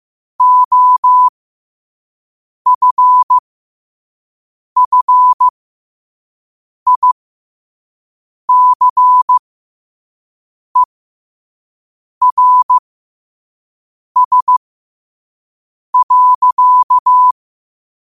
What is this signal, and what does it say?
Morse code, heard as a keyed tone, about 15 wpm, spaced as for 5 wpm, 1000 Hz. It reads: OFFICERS.
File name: qotdfiles / 5wpm